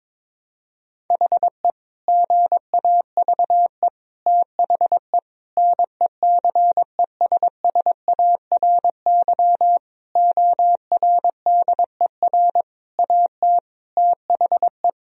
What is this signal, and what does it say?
Morse code, heard as a keyed tone, about 22 wpm, 700 Hz. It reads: HE GAVE THE NECESSARY ORDER AT THE
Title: Sentences from Top 1000 Words - Review Part 2 - 22wpm